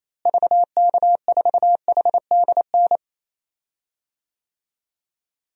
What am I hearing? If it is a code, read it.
Morse code, heard as a keyed tone, about 28 wpm, 700 Hz. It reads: VK4HDN
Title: DXCC-100 Call Signs - 6 Characters - Repeated 3x - 28wpm